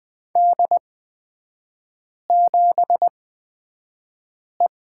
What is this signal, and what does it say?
Morse code, heard as a keyed tone, about 20 wpm, spaced as for 5 wpm, 700 Hz. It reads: D7E
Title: 224 Lesson 36e - BT - Three Characters 20wpm